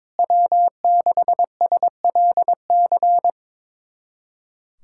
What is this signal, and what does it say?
Morse code, heard as a keyed tone, about 22 wpm, 700 Hz. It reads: W6SLC
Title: USA Callsign DE Callsign 22wpm